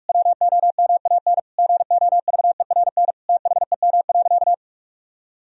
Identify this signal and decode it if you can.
Morse code, heard as a keyed tone, about 41 wpm, 700 Hz. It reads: WOMAN GOVERN THEM.
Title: Sentences from Top 1000 Words - Review Part 2 - 45wpm